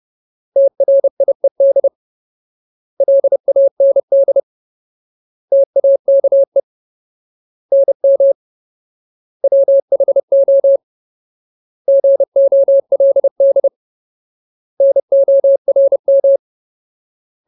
Morse code, heard as a keyed tone, about 30 wpm, 550 Hz. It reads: TRIED LAND TAKE NM WHO GOLD NORM
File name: Ultimate Word List / 30WPM Ultimate